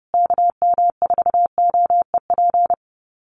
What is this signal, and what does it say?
Morse code, heard as a keyed tone, about 30 wpm, 700 Hz. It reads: KM4OEP
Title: USA Call Signs Repeated 3x 30wpm